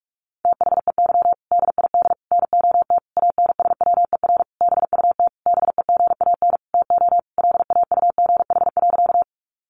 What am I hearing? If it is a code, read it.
Morse code, heard as a keyed tone, about 44 wpm, 700 Hz. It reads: THEY DID NOT ANSWER BUT BEGAN TO LAUGH.
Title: Sentences from Top 600 Words 45wpm